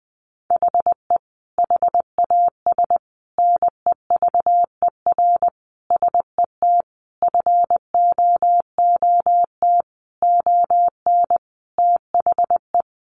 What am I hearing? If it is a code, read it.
Morse code, heard as a keyed tone, about 19 wpm, 700 Hz. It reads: HE HAS NEVER SET FOOT ON THE
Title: Sentences from Top 1500 Words 20wpm